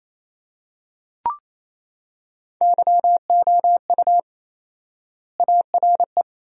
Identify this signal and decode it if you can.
Morse code, heard as a keyed tone, about 28 wpm, 700 Hz. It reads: YOU ARE
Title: 4x Word-Spacing - Sentences from Top 400 Words - 28wpm